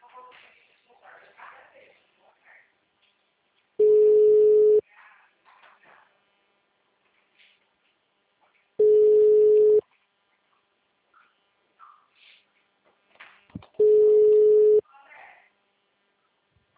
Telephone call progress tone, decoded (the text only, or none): ringback tone